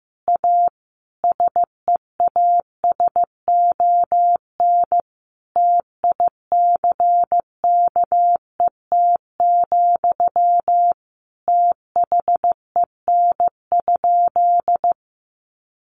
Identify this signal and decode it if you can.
Morse code, heard as a keyed tone, about 15 wpm, 700 Hz. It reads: A SEASON TICKET, THEN?
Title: Sentences from Top 1400 Words 15wpm